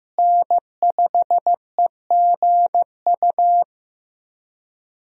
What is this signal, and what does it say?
Morse code, heard as a keyed tone, about 15 wpm, 700 Hz. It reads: N5EGU
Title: USA Call Signs - 5 Characters - Repeated 3x - 15wpm